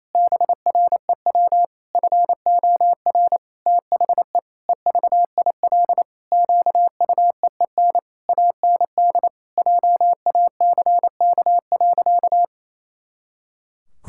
Morse code, heard as a keyed tone, about 28 wpm, 700 Hz. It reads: BREW FOR THE EVIL QUEEN AND JACK.
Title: Pangram Challenge v2 - 28wpm